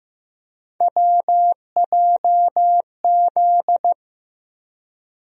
Morse code, heard as a keyed tone, about 15 wpm, 700 Hz. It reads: WJZ
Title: USA Call Sign Postfixes - 3 Letters 15wpm